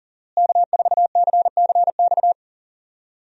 Morse code, heard as a keyed tone, about 39 wpm, 700 Hz. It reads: K4CCX
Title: USA Call Signs - 5 Characters - Repeated 3x - 40wpm